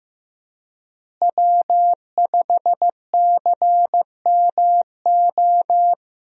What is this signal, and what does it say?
Morse code, heard as a keyed tone, about 15 wpm, 700 Hz. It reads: W5CMO
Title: USA Call Signs - 5 Characters - Repeated 2x - 15wpm